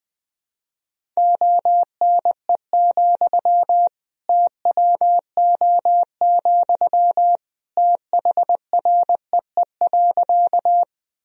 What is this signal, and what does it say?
Morse code, heard as a keyed tone, about 20 wpm, 700 Hz. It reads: ONE, TWO, THREE.
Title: Sentences from Top 1000 Words - Review Part 1 - 20wpm